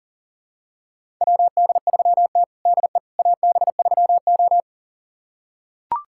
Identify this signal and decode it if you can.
Morse code, heard as a keyed tone, about 40 wpm, 700 Hz. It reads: WD3T DE AB3O
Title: USA Callsign DE Callsign - 4 Characters - 40wpm